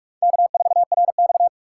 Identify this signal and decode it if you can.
Morse code, heard as a keyed tone, about 59 wpm, 700 Hz. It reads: K4RX